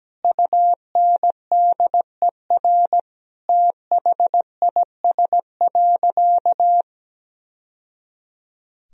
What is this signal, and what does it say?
Morse code, heard as a keyed tone, about 17 wpm, 700 Hz. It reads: UNDER THIS.
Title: Sentences from Top 1000 Words - Review Part 1 - 17wpm